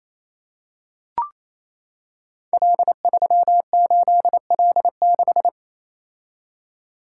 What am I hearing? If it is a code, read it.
Morse code, heard as a keyed tone, about 28 wpm, 700 Hz. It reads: L38L6
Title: Code Groups - 5 Letter:Numbers - 28wpm